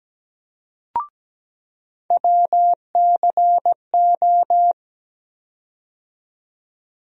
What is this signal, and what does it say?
Morse code, heard as a keyed tone, about 17 wpm, 700 Hz. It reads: WCO
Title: USA Call Sign Postfixes - 3 Letters 17wpm